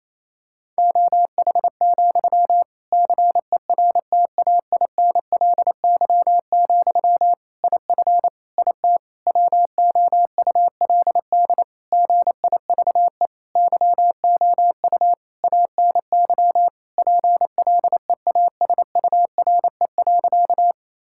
Morse code, heard as a keyed tone, about 28 wpm, 700 Hz. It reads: OH, CERTAINLY, IF IT WOULD GIVE YOU ANY PLEASURE.
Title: Sentences from Top 1600 Words 28wpm